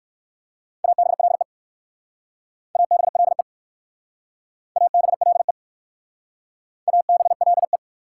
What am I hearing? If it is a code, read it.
Morse code, heard as a keyed tone, about 64 wpm, 700 Hz. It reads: ABLE ABLE ABLE ABLE